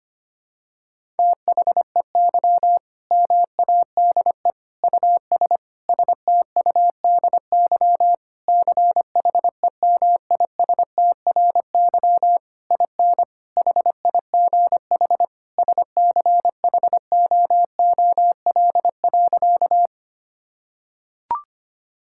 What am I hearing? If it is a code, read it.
Morse code, heard as a keyed tone, about 25 wpm, 700 Hz. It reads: THEY MADE US STUDY CHEMISTRY IN HIGH SCHOOL.
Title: Sentences from Top 1000 Words - Review Part 6 - 25wpm